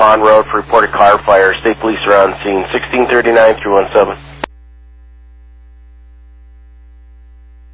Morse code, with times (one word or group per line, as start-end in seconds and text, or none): none